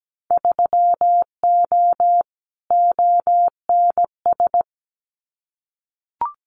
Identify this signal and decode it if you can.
Morse code, heard as a keyed tone, about 17 wpm, 700 Hz. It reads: 3O ONS
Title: Winter Field Day Exchanges 17wpm